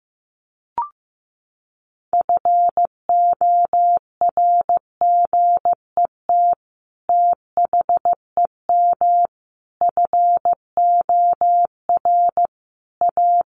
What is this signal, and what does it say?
Morse code, heard as a keyed tone, about 15 wpm, 700 Hz. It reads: FORGET THEM FOR A
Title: Sentences from Top 1200 Words 15wpm